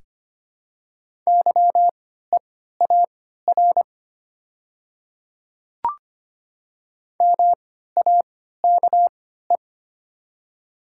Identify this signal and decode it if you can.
Morse code, heard as a keyed tone, about 25 wpm, spaced as for 14 wpm, 690 Hz. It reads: YEAR MAKE
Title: 3x Character Spacing - Top 500 Words - 25wpm